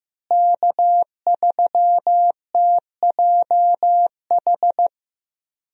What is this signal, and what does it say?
Morse code, heard as a keyed tone, about 15 wpm, 700 Hz. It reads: K3TJH
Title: USA Call Signs - 5 Characters - Repeated 3x - 15wpm